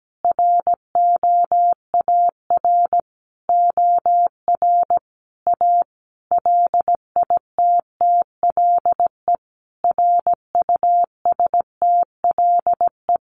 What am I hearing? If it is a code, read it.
Morse code, heard as a keyed tone, about 17 wpm, 700 Hz. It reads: ROAR OR A LITTLE RUSTLE
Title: A Sound Of Thunder 17wpm